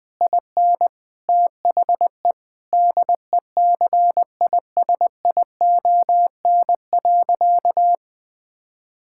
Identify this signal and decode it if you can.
Morse code, heard as a keyed tone, about 20 wpm, 700 Hz. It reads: IN THE DECISION.
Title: Sentences with 6-Word Phrases - Spoken Beginning - 20wpm